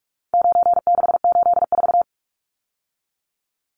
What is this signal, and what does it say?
Morse code, heard as a keyed tone, about 44 wpm, 700 Hz. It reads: 9684
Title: Numbers - 4-Digit 45wpm